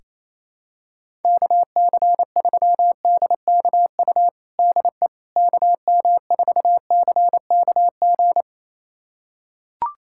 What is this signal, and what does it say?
Morse code, heard as a keyed tone, about 28 wpm, 700 Hz. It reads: KC3DKU DE KM4CKG
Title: USA Callsign DE Callsign 28wpm